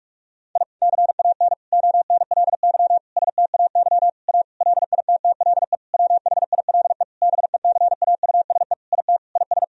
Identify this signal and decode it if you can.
Morse code, heard as a keyed tone, about 44 wpm, 700 Hz. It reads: I CAN ONLY STAY A LITTLE WHILE BECAUSE IT IS
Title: Sentences from Top 1000 Words - Review Part 3 - 45wpm